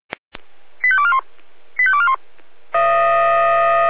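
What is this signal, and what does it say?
Five-tone selective call (ZVEI), heard as quick runs of five tones, tones of about 70 ms. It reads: 85241 85241
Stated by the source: ZVEI five-tone call 85886 received by SDR.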